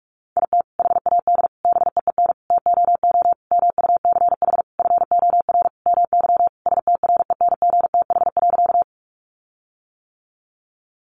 Morse code, heard as a keyed tone, about 44 wpm, 700 Hz. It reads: IT HAD BEEN TOO MUCH FOR MY STRENGTH.
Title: Sentences from Top 1300 Words 45wpm